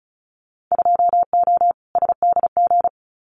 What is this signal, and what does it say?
Morse code, heard as a keyed tone, about 36 wpm, 700 Hz. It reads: 2O SDG
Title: Winter Field Day Exchanges 35wpm